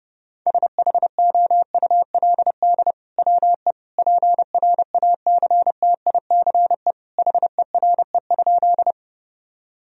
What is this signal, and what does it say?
Morse code, heard as a keyed tone, about 32 wpm, 700 Hz. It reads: SHOULD WE PRACTICE HERE?
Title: Sentences from Top 1000 Words - Review Part 4 - 30wpm